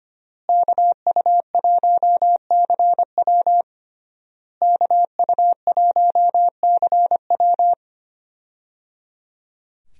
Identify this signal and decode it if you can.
Morse code, heard as a keyed tone, about 25 wpm, 700 Hz. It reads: KU1CW KU1CW
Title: USA Call Signs - 5 Characters - Repeated 3x - 25wpm